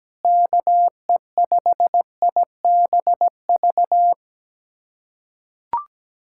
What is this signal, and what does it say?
Morse code, heard as a keyed tone, about 17 wpm, 700 Hz. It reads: KE5IBV
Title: USA Callsign DE Callsign 17wpm